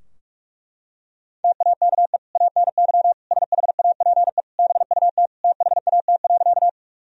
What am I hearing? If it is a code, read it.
Morse code, heard as a keyed tone, about 41 wpm, 700 Hz. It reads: TAKE ANY SHAPE BUT THAT.
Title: Sentences from Top 1000 Words - Review Part 3 - 45wpm